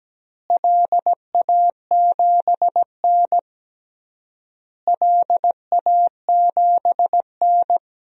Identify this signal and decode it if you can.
Morse code, heard as a keyed tone, about 17 wpm, 700 Hz. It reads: LA7N LA7N
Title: DXCC-100 Call Signs - 4 Characters - Repeated 3x - 17wpm